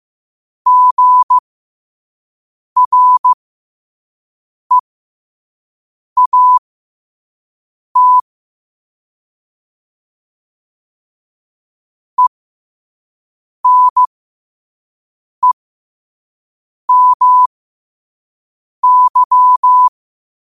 Morse code, heard as a keyed tone, about 15 wpm, spaced as for 5 wpm, 1000 Hz. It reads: GREAT ENEMY
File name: qotdfiles / 5wpm